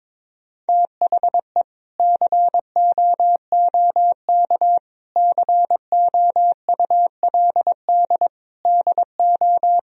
Morse code, heard as a keyed tone, about 22 wpm, 700 Hz. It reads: THE COOK COULD DO
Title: Sentences from Top 1000 Words - Review Part 4 - 22wpm